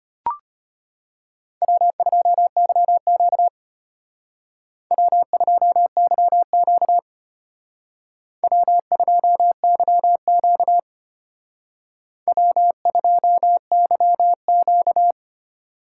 Morse code, heard as a keyed tone, about 32 wpm, 700 Hz. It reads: W2YQ W2YQ W2YQ W2YQ